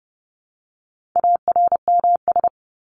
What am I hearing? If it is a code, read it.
Morse code, heard as a keyed tone, about 30 wpm, 700 Hz. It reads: ARMS
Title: ES:S Suffix - Top 100 Words with Base Words 30wpm